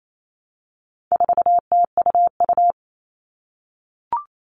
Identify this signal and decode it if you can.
Morse code, heard as a keyed tone, about 28 wpm, 700 Hz. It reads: 4TUU
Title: All Cut Numbers - 4-Digit 28wpm